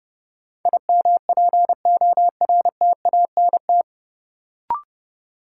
Morse code, heard as a keyed tone, about 30 wpm, 700 Hz. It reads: IMPORTANT
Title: Sets of 3 Words - New Word Drill - 30wpm